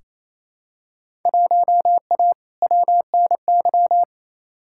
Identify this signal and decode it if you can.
Morse code, heard as a keyed tone, about 28 wpm, 700 Hz. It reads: 1A WNY
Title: Field Day Exchanges Encore - 28wpm